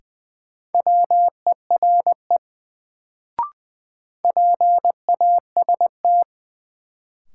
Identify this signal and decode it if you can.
Morse code, heard as a keyed tone, about 20 wpm, 700 Hz. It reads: WERE PAST
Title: Sets of 2 Words - New Word Drill - 20wpm